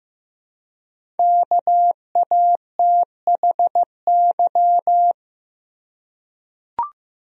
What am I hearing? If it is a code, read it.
Morse code, heard as a keyed tone, about 15 wpm, 700 Hz. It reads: KATHY
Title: USA Common Names - Max 5 Letters - 15wpm